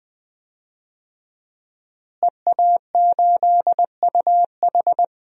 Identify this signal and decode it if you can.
Morse code, heard as a keyed tone, about 20 wpm, 700 Hz. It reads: EA8UH
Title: DXCC-100 Call Signs - 5 Characters - Repeated 2x - 20wpm